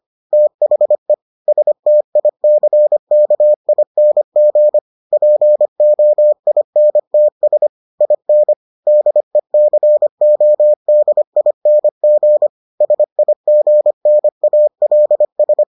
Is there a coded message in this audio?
Yes: Morse code from 0.3 to 15.6 s, heard as a keyed tone, about 25 wpm, 600 Hz. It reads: THE STICKING POINTS IN DECODING SIGNALS